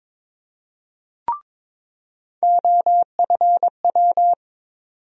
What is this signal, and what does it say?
Morse code, heard as a keyed tone, about 22 wpm, 700 Hz. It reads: OFW
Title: USA Call Sign Postfixes - 3 Letters 22wpm